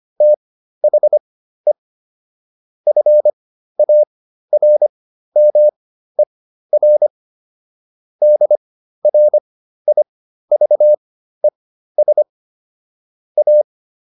Morse code, heard as a keyed tone, about 25 wpm, spaced as for 13 wpm, 600 Hz. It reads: THE FARMER DRIVES A